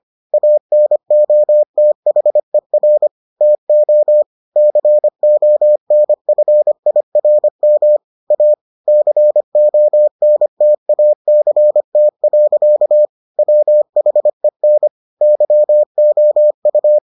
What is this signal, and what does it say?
Morse code, heard as a keyed tone, about 25 wpm, 600 Hz. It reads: ANOTHER TO CONFIRM A CONTACT. WHEN YOU